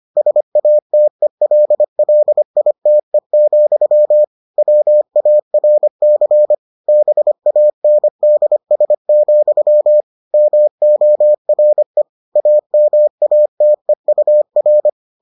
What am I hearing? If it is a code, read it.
Morse code, heard as a keyed tone, about 25 wpm, 600 Hz. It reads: SATELLITE, WARC BANDS, MORE AMATEUR